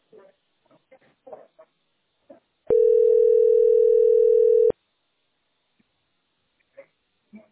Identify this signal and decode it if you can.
Ringback tone (North America), 440 + 480 Hz.